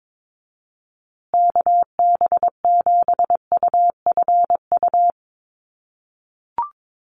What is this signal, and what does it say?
Morse code, heard as a keyed tone, about 22 wpm, 700 Hz. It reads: KB7UFU
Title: USA Call Signs Repeated 3x 22wpm